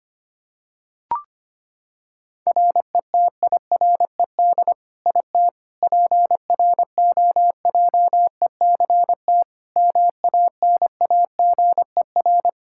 Morse code, heard as a keyed tone, about 25 wpm, 700 Hz. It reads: RETIRED IT PROJECT MANAGER
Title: Practice QSO - Exchange 3 - More Info - 25wpm